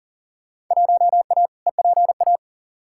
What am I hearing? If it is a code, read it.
Morse code, heard as a keyed tone, about 40 wpm, 700 Hz. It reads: 1A EPA
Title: Field Day Exchanges - 40wpm